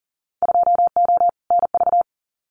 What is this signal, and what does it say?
Morse code, heard as a keyed tone, about 40 wpm, 700 Hz. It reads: 2O NV
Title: Winter Field Day Exchanges 40wpm